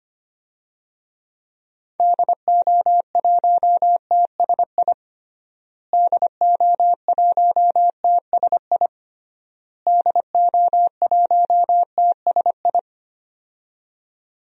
Morse code, heard as a keyed tone, about 25 wpm, 700 Hz. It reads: DO1TSI DO1TSI DO1TSI
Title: DXCC-100 Call Signs - 6 Characters - Repeated 3x - 25wpm